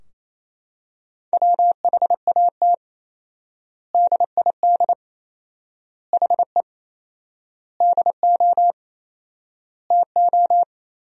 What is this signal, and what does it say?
Morse code, heard as a keyed tone, about 28 wpm, 700 Hz. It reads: WHAT DID HE DO TO
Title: 4x Word-Spacing - Sentences from Top 100 Words - 28wpm